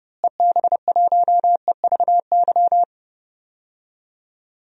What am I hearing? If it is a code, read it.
Morse code, heard as a keyed tone, about 30 wpm, 700 Hz. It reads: EB1EVY
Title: DXCC-100 Call Signs - 6 Characters - Repeated 3x - 30wpm